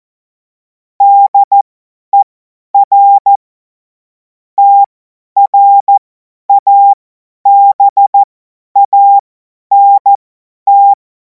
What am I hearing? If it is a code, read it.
Morse code, heard as a keyed tone, about 14 wpm, spaced as for 10 wpm, 800 Hz. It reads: DER TRABANT
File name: Morsekurs / Witze / Witz 23. Juni 2026 50 F70